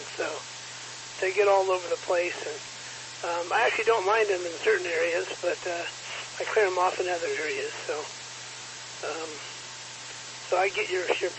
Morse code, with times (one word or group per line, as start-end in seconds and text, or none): none